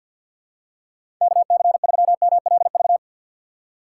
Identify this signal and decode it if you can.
Morse code, heard as a keyed tone, about 47 wpm, 700 Hz. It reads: KC3MLV
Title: USA Call Signs - 6 Characters - 3x Character Spacing - 50wpm